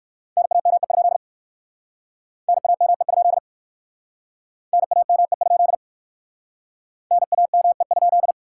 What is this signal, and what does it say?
Morse code, heard as a keyed tone, about 60 wpm, 700 Hz. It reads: NAME? NAME? NAME? NAME?